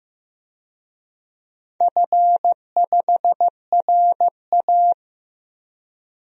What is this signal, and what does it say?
Morse code, heard as a keyed tone, about 15 wpm, 700 Hz. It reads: F5RA
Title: DXCC-100 Call Signs - 4 Characters - Repeated 2x - 15wpm